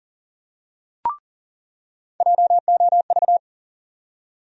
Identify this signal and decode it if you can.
Morse code, heard as a keyed tone, about 40 wpm, 700 Hz. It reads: JOV
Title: USA Call Sign Postfixes - 3 Letters 40wpm